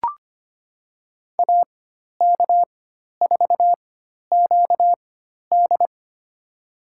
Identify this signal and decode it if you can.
Morse code, heard as a keyed tone, about 25 wpm, spaced as for 12 wpm, 700 Hz. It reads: AK4QD
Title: USA Call Signs - 5 Characters - 4x Character Spacing - 25wpm